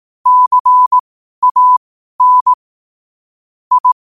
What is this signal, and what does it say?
Morse code, heard as a keyed tone, about 18 wpm, spaced as for 12 wpm, 1000 Hz. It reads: CAN I
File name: qotdfiles / 12wpm